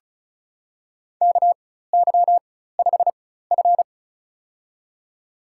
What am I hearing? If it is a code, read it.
Morse code, heard as a keyed tone, about 36 wpm, spaced as for 16 wpm, 700 Hz. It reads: KY5F
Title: USA Call Signs - 4x Character Spacing - 35wpm